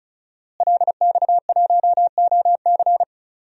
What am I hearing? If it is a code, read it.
Morse code, heard as a keyed tone, about 36 wpm, 700 Hz. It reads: LX1OC
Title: DXCC-100 Call Signs - 5 Characters - Repeated 2x - 35wpm